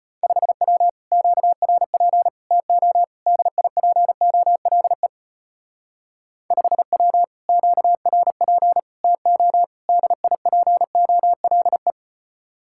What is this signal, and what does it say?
Morse code, heard as a keyed tone, about 32 wpm, 700 Hz. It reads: 5W QRP TO DIPOLE 5W QRP TO DIPOLE